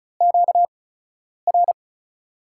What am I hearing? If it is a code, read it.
Morse code, heard as a keyed tone, about 36 wpm, 700 Hz. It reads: Q R
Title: USA Call Signs - 6 Characters - 8x Character Spacing - 35wpm